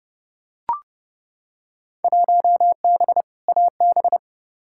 Morse code, heard as a keyed tone, about 30 wpm, 700 Hz. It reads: 1B AB
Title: Field Day Exchanges - 30wpm